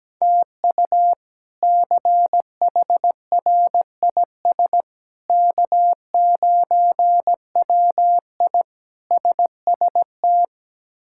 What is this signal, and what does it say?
Morse code, heard as a keyed tone, about 17 wpm, 700 Hz. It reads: TU CHRIS K9WI SST
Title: K1USN SST QSO Practice - 17wpm